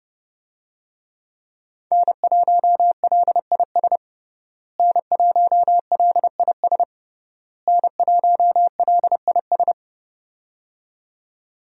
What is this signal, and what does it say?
Morse code, heard as a keyed tone, about 30 wpm, 700 Hz. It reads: N1LIS N1LIS N1LIS
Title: USA Call Signs - 5 Characters - Repeated 3x - 30wpm